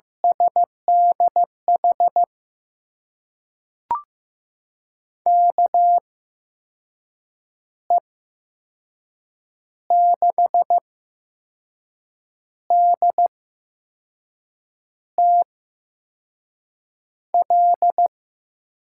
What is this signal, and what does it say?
Morse code, heard as a keyed tone, about 15 wpm, spaced as for 5 wpm, 700 Hz. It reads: SDH KE6DTL
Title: USA Call Signs - 6 Characters - 8x Character Spacing - 15wpm